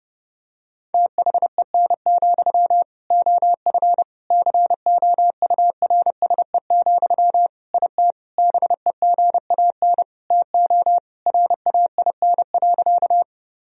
Morse code, heard as a keyed tone, about 32 wpm, 700 Hz. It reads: THEN, OF COURSE, IT BEGAN TO RAIN.
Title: Sentences from Top 1000 Words - Review Part 1 - 30wpm